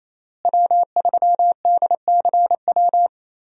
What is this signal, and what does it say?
Morse code, heard as a keyed tone, about 28 wpm, 700 Hz. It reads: W3DCW
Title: USA Call Signs - 5 Characters - Repeated 3x - 28wpm